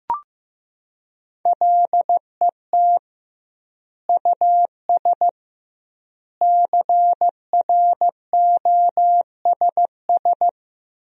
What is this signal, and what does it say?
Morse code, heard as a keyed tone, about 15 wpm, 700 Hz. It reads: LET US CROSS
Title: 2x Word-Spacing - Sentences from Top 300 Words - 15wpm